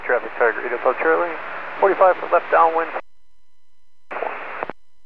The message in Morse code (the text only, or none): none